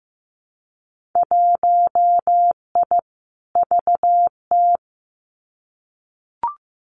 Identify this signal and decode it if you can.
Morse code, heard as a keyed tone, about 15 wpm, 700 Hz. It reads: 1I VT
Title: Winter Field Day Exchanges Encore 15wpm